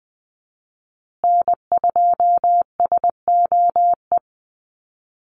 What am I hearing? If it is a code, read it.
Morse code, heard as a keyed tone, about 20 wpm, 700 Hz. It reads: N2SOE
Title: USA Call Signs Repeated 2x 20wpm